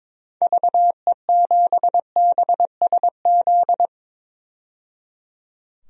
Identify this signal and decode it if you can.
Morse code, heard as a keyed tone, about 22 wpm, 700 Hz. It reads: VE7BSZ
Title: DXCC-100 Call Signs - 6 Characters - 22wpm